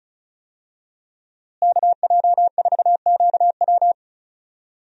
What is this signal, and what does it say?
Morse code, heard as a keyed tone, about 34 wpm, 700 Hz. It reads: KJ4QW
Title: USA Call Signs - 5 Characters - Repeated 3x - 35wpm